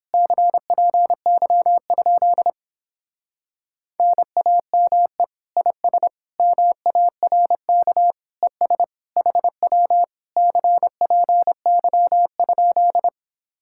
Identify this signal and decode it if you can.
Morse code, heard as a keyed tone, about 29 wpm, 700 Hz. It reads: CPY? NAME IS MARK ES HW CPY?